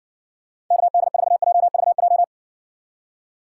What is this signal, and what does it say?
Morse code, heard as a keyed tone, about 60 wpm, 700 Hz. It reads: KD4JVJ